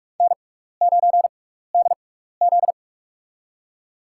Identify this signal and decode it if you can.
Morse code, heard as a keyed tone, about 44 wpm, spaced as for 15 wpm, 700 Hz. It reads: N9DZ
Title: USA Call Signs - 4 Characters - 6x Character Spacing - 45wpm